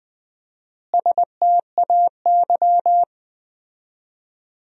Morse code, heard as a keyed tone, about 20 wpm, 700 Hz. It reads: STAY
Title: 4x Word-Spacing - Sentences from Top 500 Words - 20wpm